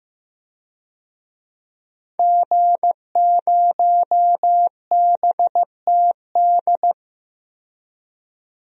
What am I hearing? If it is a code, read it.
Morse code, heard as a keyed tone, about 15 wpm, 700 Hz. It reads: G0BTD
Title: DXCC-100 Call Signs - 5 Characters - 15wpm